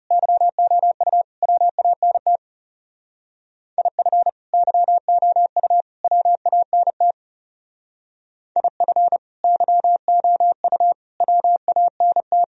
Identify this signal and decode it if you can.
Morse code, heard as a keyed tone, about 37 wpm, 700 Hz. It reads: YOU WANT IF YOU WANT IF YOU WANT
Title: Speed-Racing - Sets of 3 Words 30wpm